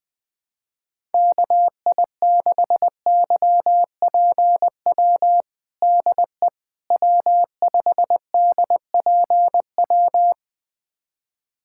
Morse code, heard as a keyed tone, about 21 wpm, 700 Hz. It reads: KI6YPW DE W5DPW
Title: USA Callsign DE Callsign 20wpm